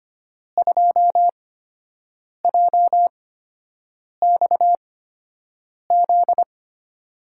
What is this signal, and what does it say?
Morse code, heard as a keyed tone, about 25 wpm, spaced as for 7 wpm, 700 Hz. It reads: 2JXZ
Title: USA Call Signs - 6 Characters - 8x Character Spacing - 25wpm